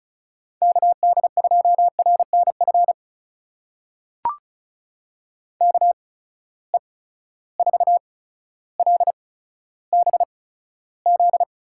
Morse code, heard as a keyed tone, about 34 wpm, spaced as for 15 wpm, 700 Hz. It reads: KD2RNF KE4LBZ
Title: USA Call Signs - 6 Characters - 8x Character Spacing - 35wpm